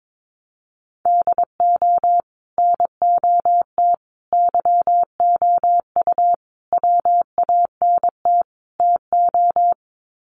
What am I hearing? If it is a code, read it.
Morse code, heard as a keyed tone, about 22 wpm, 700 Hz. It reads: DO NOT YOU WANT TO
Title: Sets of 5 Words Encore2 22wpm